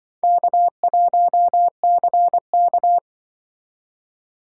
Morse code, heard as a keyed tone, about 24 wpm, 700 Hz. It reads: K1CK
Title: Speed Racing - USA 1x2 Call Signs 20wpm